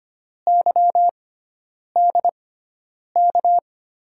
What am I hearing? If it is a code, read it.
Morse code, heard as a keyed tone, about 25 wpm, spaced as for 9 wpm, 700 Hz. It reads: YDK
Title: USA Call Signs - 6 Characters - 6x Character Spacing - 25wpm